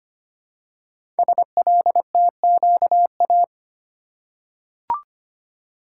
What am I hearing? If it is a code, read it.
Morse code, heard as a keyed tone, about 25 wpm, 700 Hz. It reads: SLTQA
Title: Code Groups - 5 Letters - 25wpm